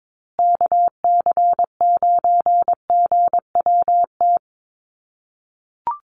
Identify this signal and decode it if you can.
Morse code, heard as a keyed tone, about 22 wpm, 700 Hz. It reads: KC9GWT
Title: USA Call Signs Repeated 3x 22wpm